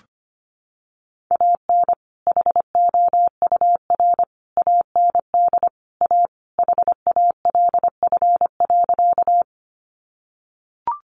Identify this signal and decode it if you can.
Morse code, heard as a keyed tone, about 25 wpm, 700 Hz. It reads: AN HOUR AND A HALF.
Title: Sentences from Top 400 Words 25wpm